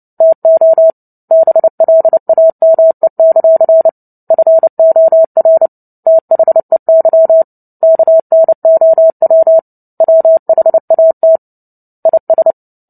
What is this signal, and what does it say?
Morse code, heard as a keyed tone, about 30 wpm, 650 Hz. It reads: TO BLAME; FOR THEY KNOW WHAT IS
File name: Book Two / 03-Chapter 3 - 29 WPM